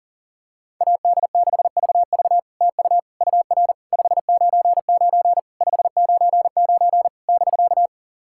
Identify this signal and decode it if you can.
Morse code, heard as a keyed tone, about 40 wpm, 660 Hz. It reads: AD6VV TU UR 599 599 <BK>
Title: SOTA QSO Practice 40wpm